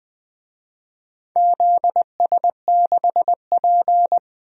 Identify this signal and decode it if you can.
Morse code, heard as a keyed tone, about 20 wpm, 700 Hz. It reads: ZS6P
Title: DXCC-100 Call Signs - 4 Characters - 20wpm